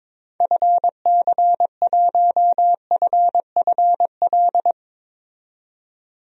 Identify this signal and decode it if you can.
Morse code, heard as a keyed tone, about 22 wpm, 700 Hz. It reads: FC1FFL
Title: DXCC-100 Call Signs - 6 Characters - 22wpm